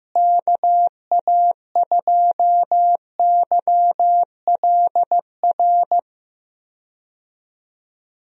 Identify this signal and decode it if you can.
Morse code, heard as a keyed tone, about 15 wpm, 700 Hz. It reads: KA2YLR
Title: USA Call Signs - 6 Characters - Repeated 3x - 15wpm